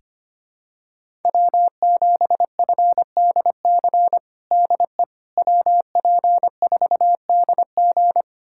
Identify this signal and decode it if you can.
Morse code, heard as a keyed tone, about 25 wpm, 700 Hz. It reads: W7FDC DE WP4DG
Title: USA Callsign DE Callsign - 5 Characters - 25wpm